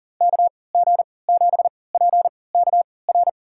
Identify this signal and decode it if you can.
Morse code, heard as a keyed tone, about 40 wpm, spaced as for 23 wpm, 700 Hz. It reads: KG7PKR
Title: USA Call Signs - 3x Character Spacing - 40wpm